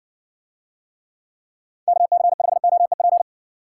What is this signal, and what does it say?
Morse code, heard as a keyed tone, about 54 wpm, 700 Hz. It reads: KC5OEP